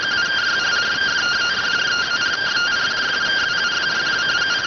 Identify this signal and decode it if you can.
Two-tone teleprinter signal, about 170 Hz shift.